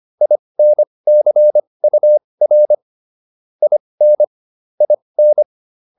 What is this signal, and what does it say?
Morse code, heard as a keyed tone, about 25 wpm, spaced as for 20 wpm, 600 Hz. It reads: INCUR IN IN